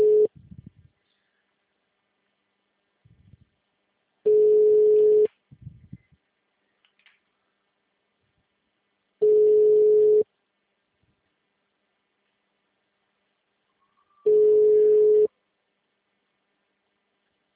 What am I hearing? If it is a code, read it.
Ringback tone (Europe, 425 Hz).